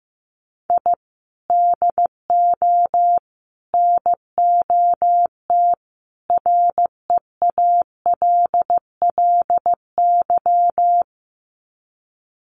Morse code, heard as a keyed tone, about 15 wpm, 700 Hz. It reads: I DO NOT REALLY
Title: Sets of 4 Words 15wpm